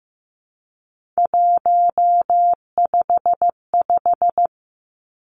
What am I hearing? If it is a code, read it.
Morse code, heard as a keyed tone, about 15 wpm, 700 Hz. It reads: 155
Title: Three Challenging Letter-Number Combinations 15wpm